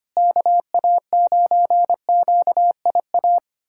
Morse code, heard as a keyed tone, about 25 wpm, 700 Hz. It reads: KA9QIA
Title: USA Callsign DE Callsign 25wpm